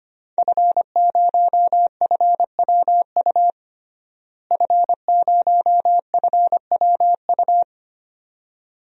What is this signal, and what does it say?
Morse code, heard as a keyed tone, about 25 wpm, 700 Hz. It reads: F0FWU F0FWU
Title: DXCC-100 Call Signs - 5 Characters - Repeated 2x - 25wpm